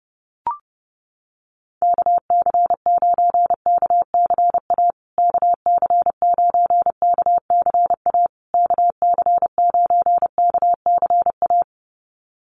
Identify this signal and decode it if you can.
Morse code, heard as a keyed tone, about 30 wpm, 700 Hz. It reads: KC9KCA KC9KCA KC9KCA
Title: USA Call Signs Repeated 3x 30wpm